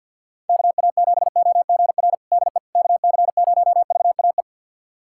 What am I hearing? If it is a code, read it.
Morse code, heard as a keyed tone, about 51 wpm, 700 Hz. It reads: KA7OGR DE KC0VAE
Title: USA Callsign DE Callsign - 6 Characters - 50wpm